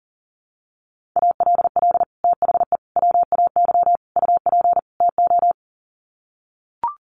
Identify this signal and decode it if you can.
Morse code, heard as a keyed tone, about 40 wpm, 700 Hz. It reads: ALL THE WAY UP TO
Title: Sets of 5 Words Encore 40wpm